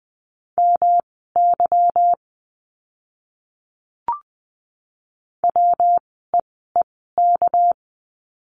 Morse code, heard as a keyed tone, about 20 wpm, spaced as for 15 wpm, 700 Hz. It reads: MY WEEK
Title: 230 Lesson 37f - Letter x - Words 20wpm